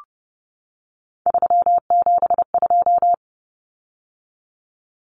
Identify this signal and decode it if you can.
Morse code, heard as a keyed tone, about 30 wpm, 700 Hz. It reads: 372
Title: Common Cut Numbers - 3-Digit 30wpm